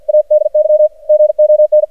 Morse code, heard as a keyed tone, about 53 wpm, 610 Hz. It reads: ADY GON
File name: archived / CRSnet